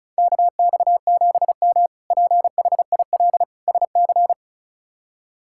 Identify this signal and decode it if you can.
Morse code, heard as a keyed tone, about 36 wpm, 700 Hz. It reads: KX7M PHIL SC
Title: K1USN SST QSO Practice - 35wpm